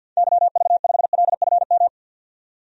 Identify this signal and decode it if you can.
Morse code, heard as a keyed tone, about 51 wpm, 700 Hz. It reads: YV5LFM
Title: DXCC-100 Call Signs - 6 Characters - 50wpm